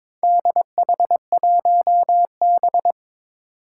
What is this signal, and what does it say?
Morse code, heard as a keyed tone, about 22 wpm, 700 Hz. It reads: DH1B
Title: Four Challenging Letter-Number Combinations 22wpm